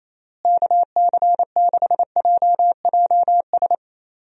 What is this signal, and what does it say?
Morse code, heard as a keyed tone, about 28 wpm, 700 Hz. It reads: KC6JJS
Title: USA Call Signs - 6 Characters - Repeated 2x - 28wpm